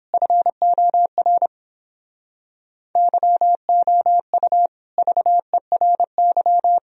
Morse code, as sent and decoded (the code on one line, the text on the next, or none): ..-. --- .-. / -.-- --- ..- / ...- . .-. -.--
FOR YOU VERY